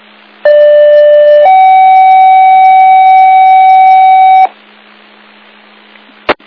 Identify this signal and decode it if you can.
Two-tone page, heard as a one-second tone, then a three-second tone, about 590 then 760 Hz. It is heard from 0.4 to 4.5 s.